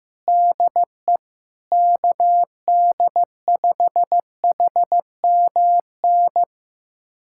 Morse code, heard as a keyed tone, about 15 wpm, 700 Hz. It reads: DE KD5HMN
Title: USA Callsign DE Callsign - 6 Characters - 15wpm